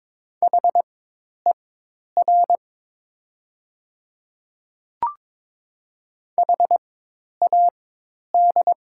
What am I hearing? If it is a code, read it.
Morse code, heard as a keyed tone, about 22 wpm, spaced as for 10 wpm, 700 Hz. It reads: HER HAD
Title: 4x Character Spacing - Top 100 Words - 22wpm